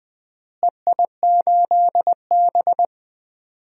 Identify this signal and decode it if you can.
Morse code, heard as a keyed tone, about 20 wpm, 700 Hz. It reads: EI8B
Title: DXCC-100 Call Signs - 4 Characters - Repeated 3x - 20wpm